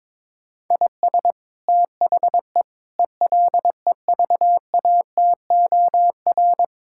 Morse code, heard as a keyed tone, about 22 wpm, 700 Hz. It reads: IS THE ELEVATOR
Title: Sentences from Top 1600 Words 22wpm